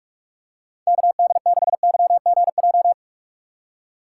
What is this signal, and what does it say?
Morse code, heard as a keyed tone, about 47 wpm, 700 Hz. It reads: KD6YGJ
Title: USA Call Signs - 6 Characters - 6x Character Spacing - 45wpm